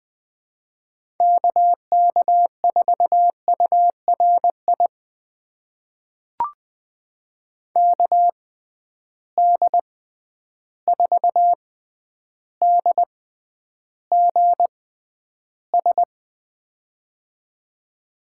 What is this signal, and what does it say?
Morse code, heard as a keyed tone, about 20 wpm, spaced as for 10 wpm, 700 Hz. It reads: KK4URI KD4DGS
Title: USA Call Signs - 6 Characters - 6x Character Spacing - 20wpm